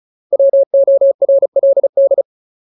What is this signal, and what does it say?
Morse code, heard as a keyed tone, about 37 wpm, 550 Hz. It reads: WORLD
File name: Ultimate Word List / 35WPM Ultimate